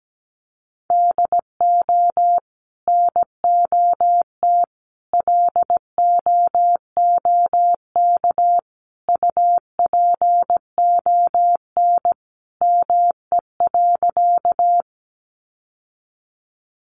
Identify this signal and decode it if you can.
Morse code, heard as a keyed tone, about 17 wpm, 700 Hz. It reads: DO NOT LOOK UPON ME.
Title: Sentences from Top 1100 Words 17wpm